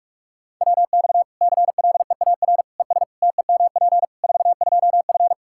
Morse code, heard as a keyed tone, about 45 wpm, 700 Hz. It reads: WX CLEAR ES TEMP 42F
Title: Practice QSO - Exchange 2 - Info - 45wpm